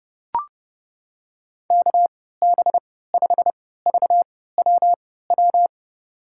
Morse code, heard as a keyed tone, about 30 wpm, spaced as for 17 wpm, 700 Hz. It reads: KB5VWW
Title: USA Call Signs - 6 Characters - 3x Character Spacing - 30wpm